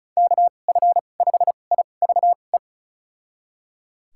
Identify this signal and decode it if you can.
Morse code, heard as a keyed tone, about 34 wpm, spaced as for 25 wpm, 700 Hz. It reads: KF5IVE
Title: USA Call Signs - 2x Character Spacing - 35wpm